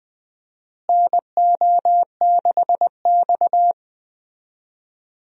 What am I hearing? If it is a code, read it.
Morse code, heard as a keyed tone, about 21 wpm, 700 Hz. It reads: NO6X
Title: USA Call Signs - 4 Characters - Repeated 3x - 20wpm